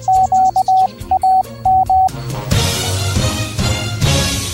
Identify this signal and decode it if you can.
Morse code, heard as a keyed tone, about 20 wpm, 760 Hz. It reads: QAM